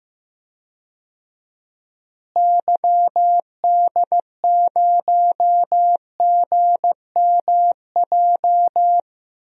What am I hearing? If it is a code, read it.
Morse code, heard as a keyed tone, about 15 wpm, 700 Hz. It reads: YD0GMJ
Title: DXCC-100 Call Signs - 6 Characters - Repeated 2x - 15wpm